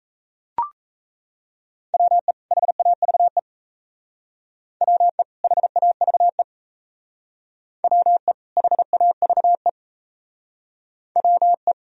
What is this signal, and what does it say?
Morse code, heard as a keyed tone, about 34 wpm, 700 Hz. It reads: WE HAVE WE HAVE WE HAVE WE
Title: Speed-Racing - Sets of 2 Words 28wpm